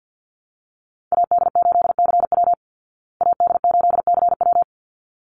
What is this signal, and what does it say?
Morse code, heard as a keyed tone, about 48 wpm, 700 Hz. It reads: AD8CW AD8CW
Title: Call Signs Repeated 3x - 50wpm